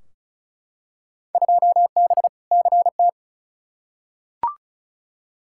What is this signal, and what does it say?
Morse code, heard as a keyed tone, about 36 wpm, 700 Hz. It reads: 2B CT
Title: Field Day Exchanges Encore - 35wpm